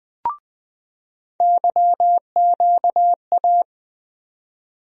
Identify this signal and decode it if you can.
Morse code, heard as a keyed tone, about 21 wpm, 700 Hz. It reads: YQA
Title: USA Call Sign Postfixes - 3 Letters 20wpm